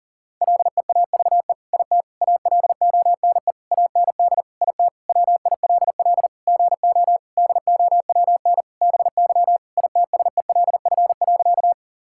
Morse code, heard as a keyed tone, about 40 wpm, 700 Hz. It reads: LEAVE IT ALONE AND IT WILL GO DOWN BY ITSELF.
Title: Sentences from Top 1000 Words - Review Part 5 - 40wpm